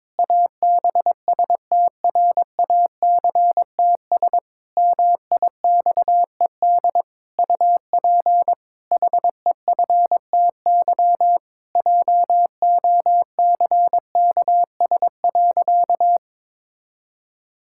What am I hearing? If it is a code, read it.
Morse code, heard as a keyed tone, about 22 wpm, 700 Hz. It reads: ABSTRACTS MIXED UP HEFTY JOCKS.
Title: Pangram Challenge v2 - 22wpm